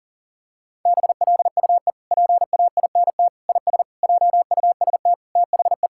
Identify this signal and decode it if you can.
Morse code, heard as a keyed tone, about 40 wpm, 700 Hz. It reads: BLUE PAINT IS JUST THE
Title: Sentences from Top 1000 Words - Review Part 6 - 40wpm